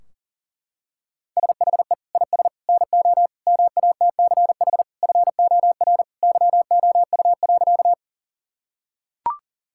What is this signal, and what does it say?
Morse code, heard as a keyed tone, about 40 wpm, 700 Hz. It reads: SHE IS NO MATCH FOR YOU.
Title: Sentences from Top 1600 Words 40wpm